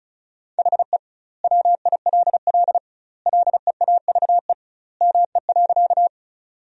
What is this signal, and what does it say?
Morse code, heard as a keyed tone, about 34 wpm, 700 Hz. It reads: HE WILL LEAVE ME.
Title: 2x Word-Spacing - Sentences from Top 400 Words - 35wpm